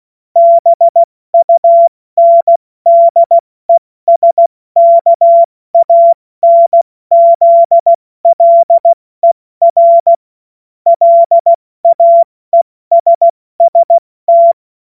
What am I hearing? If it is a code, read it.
Morse code, heard as a keyed tone, about 16 wpm, spaced as for 14 wpm, 680 Hz. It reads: BUNDESKANZLER LAESST
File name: Morsekurs / Witze / Witz 03 Juni Mai 2025 70 F80